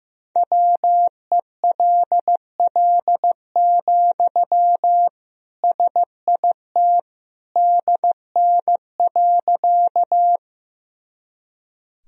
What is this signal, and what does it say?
Morse code, heard as a keyed tone, about 14 wpm, 700 Hz. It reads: WELL, SIT DN.
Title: Sentences from Top 500 Words and Abbreviations 15wpm